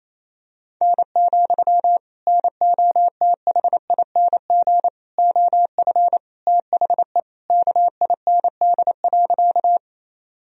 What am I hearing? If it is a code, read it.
Morse code, heard as a keyed tone, about 28 wpm, 700 Hz. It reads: N, NOTHING OF THE KIND.
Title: Sentences from Top 500 Words and Abbreviations 28wpm